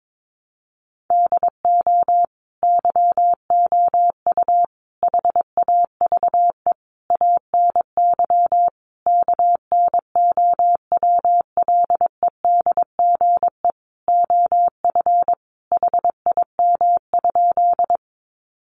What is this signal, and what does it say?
Morse code, heard as a keyed tone, about 22 wpm, 700 Hz. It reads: DO YOU HAVE ANY KNOWLEDGE OF HIM?
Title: Sentences from Top 600 Words 22wpm